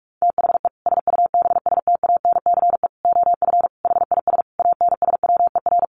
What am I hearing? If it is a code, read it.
Morse code, heard as a keyed tone, about 44 wpm, 700 Hz. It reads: THE SUBSTANCE OF HIS ANSWER
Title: Sentences from Top 1500 Words 45wpm